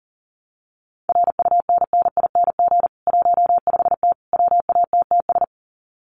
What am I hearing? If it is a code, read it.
Morse code, heard as a keyed tone, about 40 wpm, 700 Hz. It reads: RUNNING 15T WATTS
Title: Common QSO Phrases 40wpm